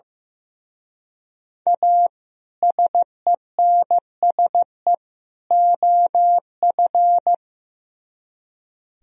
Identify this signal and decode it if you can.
Morse code, heard as a keyed tone, about 17 wpm, 700 Hz. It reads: A SENSE OF
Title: Speed-Racing - Sets of 3 Words 15wpm